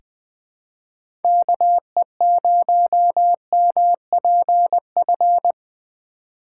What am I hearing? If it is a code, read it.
Morse code, heard as a keyed tone, about 20 wpm, 700 Hz. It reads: KE0MPF
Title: USA Call Signs - 4x Character Spacing - 20wpm